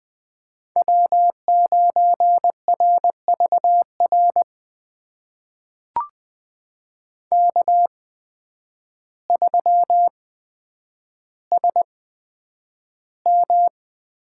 Morse code, heard as a keyed tone, about 20 wpm, spaced as for 9 wpm, 700 Hz. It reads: W9RVR K3SM
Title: USA Call Signs - 8x Character Spacing - 20wpm